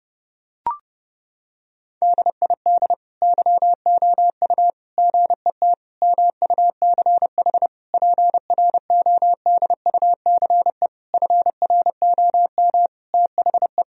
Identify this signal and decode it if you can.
Morse code, heard as a keyed tone, about 30 wpm, 700 Hz. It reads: DID YOU GET MUCH PRODUCE FROM THE
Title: Sentences from Top 1000 Words - Review Part 3 - 30wpm